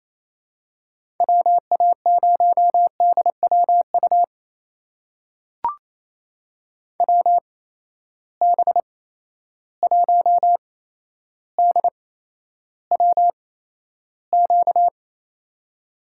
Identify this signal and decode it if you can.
Morse code, heard as a keyed tone, about 28 wpm, spaced as for 12 wpm, 700 Hz. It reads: WA0DWU WB1DWQ
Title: USA Call Signs - 6 Characters - 8x Character Spacing - 28wpm